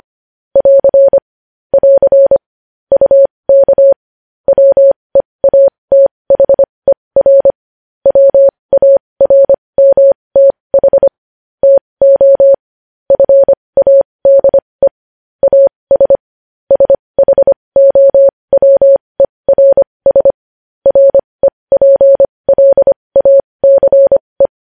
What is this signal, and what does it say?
Morse code, heard as a keyed tone, about 25 wpm, spaced as for 20 wpm, 550 Hz. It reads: <AR> <AR> UK WEATHER WARMTH TO FADE AS SHOWERS REPLACE
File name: podcast / News headlines-1x-25x20